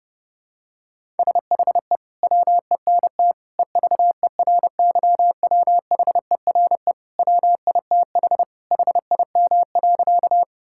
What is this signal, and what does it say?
Morse code, heard as a keyed tone, about 30 wpm, 700 Hz. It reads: SHE WENT EVERYWHERE WITH HIM.
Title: Sentences from Top 1600 Words 30wpm